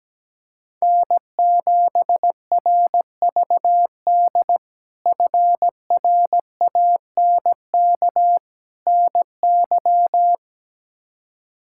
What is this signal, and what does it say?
Morse code, heard as a keyed tone, about 17 wpm, 700 Hz. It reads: N7RVD FRANK NY
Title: K1USN SST QSO Practice - 17wpm